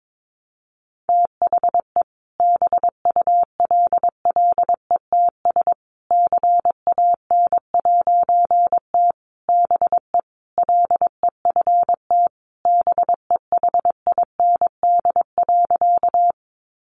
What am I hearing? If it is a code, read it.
Morse code, heard as a keyed tone, about 22 wpm, 700 Hz. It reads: THE BULLETS CAN'T BE LEFT BEHIND.
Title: A Sound Of Thunder 22wpm